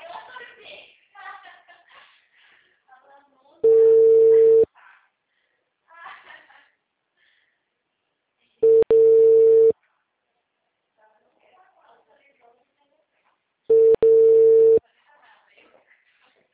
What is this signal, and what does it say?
Ringback tone (China, 450 Hz), with speech.